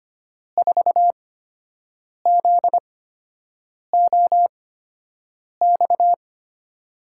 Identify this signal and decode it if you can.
Morse code, heard as a keyed tone, about 25 wpm, spaced as for 7 wpm, 700 Hz. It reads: 4ZOX
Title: USA Call Signs - 6 Characters - 8x Character Spacing - 25wpm